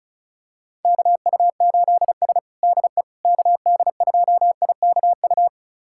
Morse code, heard as a keyed tone, about 36 wpm, 690 Hz. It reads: KU8S DE KD2IKU
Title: USA Callsign DE Callsign 35wpm